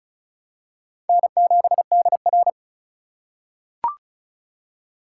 Morse code, heard as a keyed tone, about 36 wpm, 700 Hz. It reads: N7DR
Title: K1USN SST QSO Practice - 35wpm